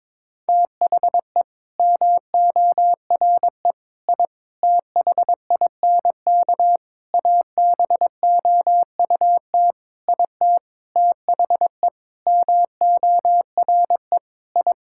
Morse code, heard as a keyed tone, about 22 wpm, 700 Hz. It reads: THE MORE I THINK ABOUT IT THE MORE I
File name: Sentences with 9-Word Phrases - 22wpm